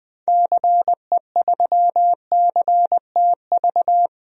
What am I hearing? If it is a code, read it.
Morse code, heard as a keyed tone, about 20 wpm, 700 Hz. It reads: CE3CTV
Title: DXCC-100 Call Signs - 6 Characters - 20wpm